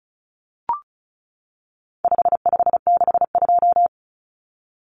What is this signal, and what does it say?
Morse code, heard as a keyed tone, about 34 wpm, 700 Hz. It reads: 5562
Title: Common Cut Numbers - 4-Digit 35wpm